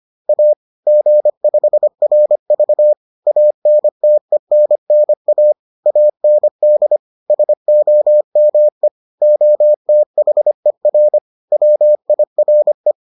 Morse code, heard as a keyed tone, about 25 wpm, 600 Hz. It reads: A G5RV ANTENNA AND SOME OTHER WIRE